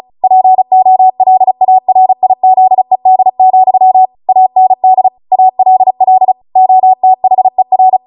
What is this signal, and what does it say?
Morse code, heard as a keyed tone, about 36 wpm, 750 Hz. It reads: POLARIZED, AND ALL OTHER